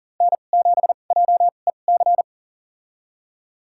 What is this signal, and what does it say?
Morse code, heard as a keyed tone, about 40 wpm, spaced as for 29 wpm, 700 Hz. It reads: N7JEC
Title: USA Call Signs - 5 Characters - 2x Character Spacing - 40wpm